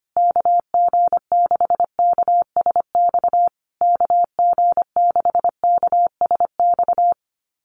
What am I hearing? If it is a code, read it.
Morse code, heard as a keyed tone, about 25 wpm, 700 Hz. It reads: KG6KSX KG6KSX
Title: USA Call Signs Repeated 3x 25wpm